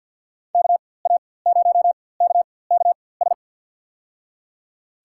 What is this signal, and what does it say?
Morse code, heard as a keyed tone, about 51 wpm, spaced as for 23 wpm, 700 Hz. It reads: KA0KKS
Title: USA Call Signs - 6 Characters - 4x Character Spacing - 50wpm